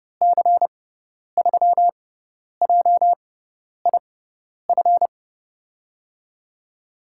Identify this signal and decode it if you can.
Morse code, heard as a keyed tone, about 31 wpm, spaced as for 10 wpm, 700 Hz. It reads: C3JIF
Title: USA Call Signs - 6 Characters - 6x Character Spacing - 30wpm